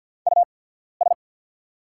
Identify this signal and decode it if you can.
Morse code, heard as a keyed tone, about 48 wpm, spaced as for 14 wpm, 700 Hz. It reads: US
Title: 8x Character Spacing - Top 300 Words - 50wpm